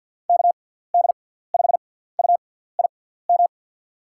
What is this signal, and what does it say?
Morse code, heard as a keyed tone, about 52 wpm, spaced as for 17 wpm, 700 Hz. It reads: KD5UIM
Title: USA Call Signs - 6x Character Spacing - 50wpm